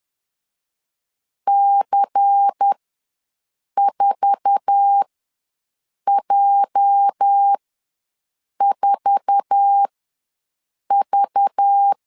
Morse code, heard as a keyed tone, about 10 wpm, spaced as for 6 wpm, 800 Hz. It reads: C4J4V
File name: LEZIONI / 8 YJ45 / 8 YJ4 ws 10wpm 4020